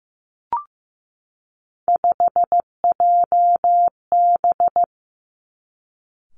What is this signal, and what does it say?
Morse code, heard as a keyed tone, about 15 wpm, 700 Hz. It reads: 5JB
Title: Three Challenging Letter-Number Combinations 15wpm